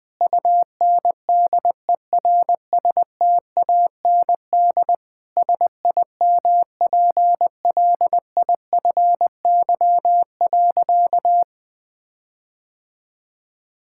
Morse code, heard as a keyed tone, about 21 wpm, 700 Hz. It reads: UNDERSTAND SIMPLIFY.
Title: Sentences with 7-Word Phrases - 20wpm